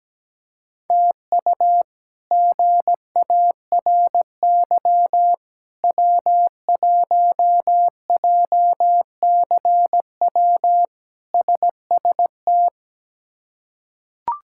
Morse code, heard as a keyed tone, about 17 wpm, 700 Hz. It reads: TU GARY W1JCW SST
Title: K1USN SST QSO Practice - 17wpm